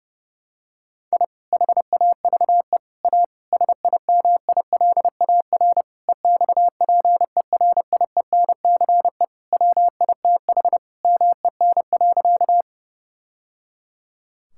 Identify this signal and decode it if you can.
Morse code, heard as a keyed tone, about 32 wpm, 700 Hz. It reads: I HAVE A SIMILAR EXPERIENCE WITH MEN.
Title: Sentences from Top 1000 Words - Review Part 5 - 30wpm